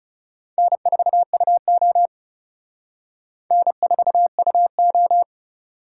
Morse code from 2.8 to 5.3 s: N4UO

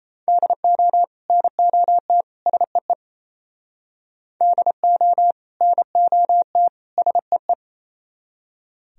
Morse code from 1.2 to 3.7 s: NOT SEE

